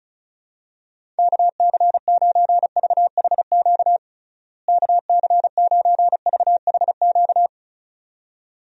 Morse code, heard as a keyed tone, about 36 wpm, 700 Hz. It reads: KC9VHQ KC9VHQ
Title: USA Call Signs - 6 Characters - Repeated 2x - 35wpm